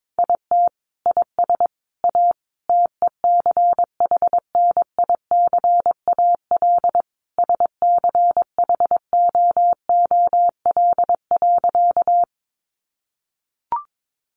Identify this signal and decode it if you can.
Morse code, heard as a keyed tone, about 22 wpm, 700 Hz. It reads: IT IS A TECHNICAL SCHOOL.
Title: Sentences from Top 1100 Words 22wpm